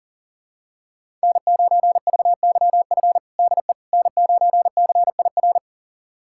Morse code, heard as a keyed tone, about 40 wpm, 700 Hz. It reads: N9VYF DE N9CIR
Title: USA Callsign DE Callsign - 5 Characters - 40wpm